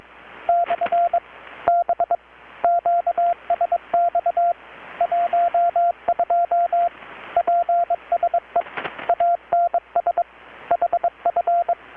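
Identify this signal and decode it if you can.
Morse code, heard as a keyed tone, about 21 wpm, 690 Hz, with radio noise. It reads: / B QSX 12 PSE ANS HF
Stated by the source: Morse from KPH's Night of Nights XX maritime radiotelegraphy event, received in Chiapas, Mexico about 3,800 km away on a Kenwood R-600 with a grounded 10 m long-wire antenna.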